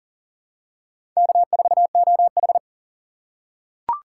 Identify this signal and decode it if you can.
Morse code, heard as a keyed tone, about 40 wpm, 700 Hz. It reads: K4OH
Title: USA Call Signs - 4 Characters - 6x Character Spacing - 40wpm